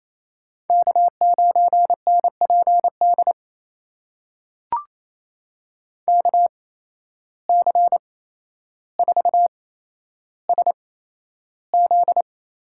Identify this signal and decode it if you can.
Morse code, heard as a keyed tone, about 27 wpm, spaced as for 12 wpm, 700 Hz. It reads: K9NPD KC4SZ
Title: USA Call Signs - 8x Character Spacing - 28wpm